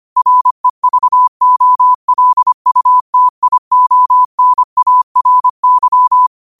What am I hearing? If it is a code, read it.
Morse code, heard as a keyed tone, about 25 wpm, 1000 Hz. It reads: REVOLUTIONARY
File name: qotdfiles / 25wpm